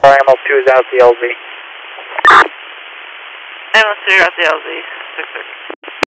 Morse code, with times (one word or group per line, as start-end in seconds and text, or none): none